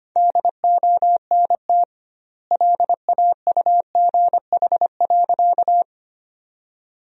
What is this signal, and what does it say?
Morse code, heard as a keyed tone, about 25 wpm, 700 Hz. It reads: DONT LAUGH.
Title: 2x Word-Spacing - Sentences from Top 500 Words - 25wpm